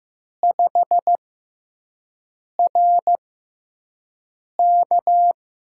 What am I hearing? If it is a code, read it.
Morse code, heard as a keyed tone, about 15 wpm, spaced as for 5 wpm, 700 Hz. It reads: 5RK
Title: USA Call Signs - 5 Characters - 6x Character Spacing - 15wpm